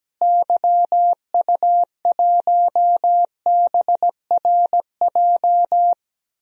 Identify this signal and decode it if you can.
Morse code, heard as a keyed tone, about 17 wpm, 700 Hz. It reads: YU1BRJ
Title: DXCC-100 Call Signs - 6 Characters - Repeated 2x - 17wpm